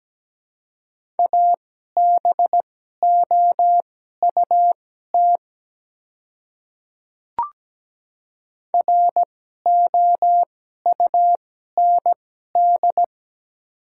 Morse code, heard as a keyed tone, about 17 wpm, spaced as for 12 wpm, 700 Hz. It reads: ABOUT ROUND
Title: 2x Character Spacing - Top 200 Words - 17wpm